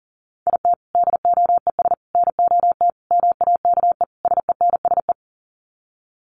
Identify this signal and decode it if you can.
Morse code, heard as a keyed tone, about 40 wpm, 700 Hz. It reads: IT DOES NOT MAKE SENSE
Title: Sets of 5 Words Encore2 40wpm